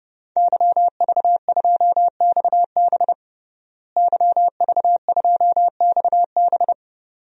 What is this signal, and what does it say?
Morse code, heard as a keyed tone, about 30 wpm, 700 Hz. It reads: YV2XB YV2XB
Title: DXCC-100 Call Signs - 5 Characters - Repeated 2x - 30wpm